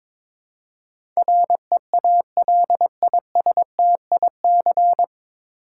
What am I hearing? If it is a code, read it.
Morse code, heard as a keyed tone, about 22 wpm, 690 Hz. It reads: REALISTIC
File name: 9 Letter Words - First 3 Letters Given - 22wpm